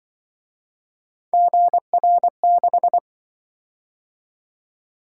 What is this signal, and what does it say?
Morse code, heard as a keyed tone, about 24 wpm, 700 Hz. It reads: GR6
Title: Speed-Racing - Three Letter-Number Combinations - 20wpm